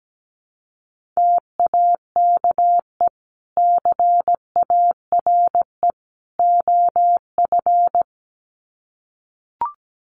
Morse code, heard as a keyed tone, about 17 wpm, 700 Hz. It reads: TAKE CARE OF
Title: Sets of 3 Words 17wpm